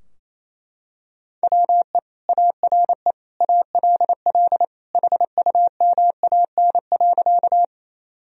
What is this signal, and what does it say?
Morse code, heard as a keyed tone, about 28 wpm, 700 Hz. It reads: WE ARE ALL HUMAN.
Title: Sentences from Top 1000 Words - Review Part 3 - 28wpm